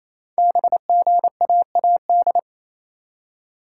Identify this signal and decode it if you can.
Morse code, heard as a keyed tone, about 29 wpm, 700 Hz. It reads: BGAAD
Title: Code Groups - 5 Letters - 28wpm